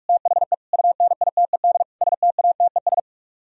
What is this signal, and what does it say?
Morse code, heard as a keyed tone, about 44 wpm, 700 Hz. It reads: THE UNITED STATES